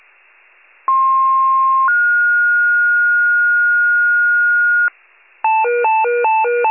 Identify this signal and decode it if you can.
Two-tone page, heard as a one-second tone, then a three-second tone, about 1060 then 1430 Hz, with noise. It is heard from 0.9 to 4.9 s.
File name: wcalert